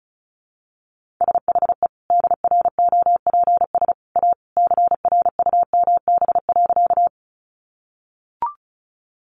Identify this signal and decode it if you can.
Morse code, heard as a keyed tone, about 34 wpm, 700 Hz. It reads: SHE DROPS A CRUMB.
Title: Additional Easy Phrases 35wpm